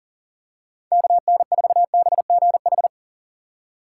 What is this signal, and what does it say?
Morse code, heard as a keyed tone, about 38 wpm, 700 Hz. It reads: KN4BGH
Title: USA Call Signs - 6 Characters - 4x Character Spacing - 40wpm